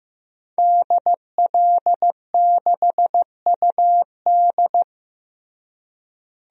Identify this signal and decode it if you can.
Morse code, heard as a keyed tone, about 15 wpm, 700 Hz. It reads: DL6UD
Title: DXCC-100 Call Signs - 5 Characters - Repeated 3x - 15wpm